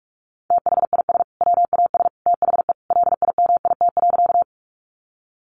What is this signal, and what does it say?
Morse code, heard as a keyed tone, about 44 wpm, 700 Hz. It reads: THIS WAS THE LIMIT.
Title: Sentences from Top 1500 Words 45wpm